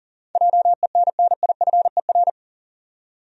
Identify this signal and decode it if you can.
Morse code, heard as a keyed tone, about 37 wpm, 700 Hz. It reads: JENNIFER
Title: USA Common Names - 40wpm